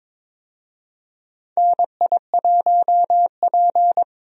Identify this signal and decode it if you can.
Morse code, heard as a keyed tone, about 23 wpm, 700 Hz. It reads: NI1P
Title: USA Call Signs - 4 Characters - Repeated 3x - 22wpm